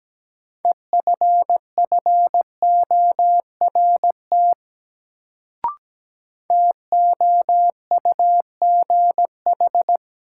Morse code, heard as a keyed tone, about 17 wpm, 700 Hz. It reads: EFFORT TOUGH
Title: Sentences from Top 1200 Words - New Word Drill Alt - 17wpm